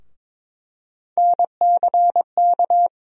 Morse code, heard as a keyed tone, about 22 wpm, 700 Hz. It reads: NCK
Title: USA Call Sign Postfixes - 3 Letters 22wpm